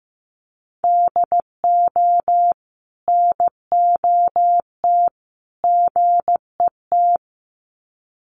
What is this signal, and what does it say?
Morse code, heard as a keyed tone, about 15 wpm, 700 Hz. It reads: DO NOT GET
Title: Sets of 3 Words 15wpm